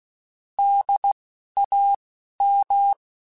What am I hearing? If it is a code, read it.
Morse code, heard as a keyed tone, about 16 wpm, 800 Hz. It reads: D A M
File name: Alphabetical / Alphabetical mp3s / Ds @10